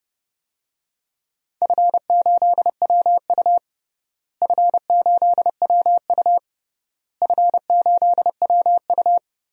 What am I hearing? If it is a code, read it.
Morse code, heard as a keyed tone, about 29 wpm, 700 Hz. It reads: F8WU F8WU F8WU
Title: DXCC-100 Call Signs - 4 Characters - Repeated 3x - 30wpm